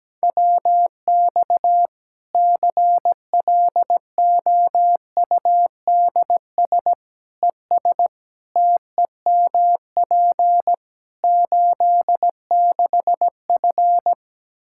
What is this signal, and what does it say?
Morse code, heard as a keyed tone, about 17 wpm, 700 Hz. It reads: WX CLOUDS ES TEMP 86F
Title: Practice QSO - Exchange 2 - Info - 17wpm